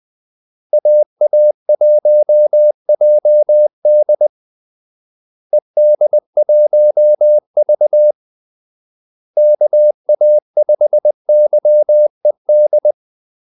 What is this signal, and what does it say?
Morse code, heard as a keyed tone, about 20 wpm, 600 Hz. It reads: AA1JD ED1V KA5YED
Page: Kaldesignaler 20 wpm